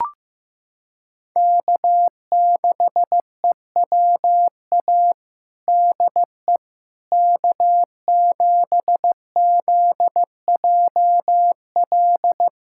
Morse code, heard as a keyed tone, about 15 wpm, 700 Hz. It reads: K6EWA DE K7ZJL
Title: USA Callsign DE Callsign - 5 Characters - 15wpm